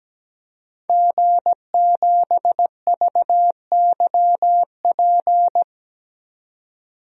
Morse code, heard as a keyed tone, about 17 wpm, 700 Hz. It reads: G7VYP
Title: DXCC-100 Call Signs - 5 Characters - Repeated 2x - 17wpm